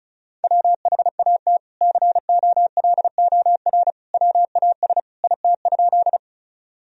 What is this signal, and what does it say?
Morse code, heard as a keyed tone, about 36 wpm, 700 Hz. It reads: WHAT COLOR WAS IT?
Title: Sentences from Top 1000 Words - Review Part 2 - 35wpm